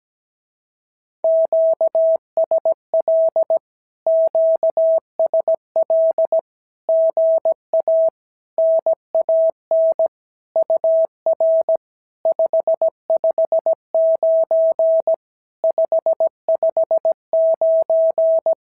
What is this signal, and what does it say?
Morse code, heard as a keyed tone, about 17 wpm, 650 Hz. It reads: QSL QSL GA NAN UR 559 559
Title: SOTA QSO Practice 17wpm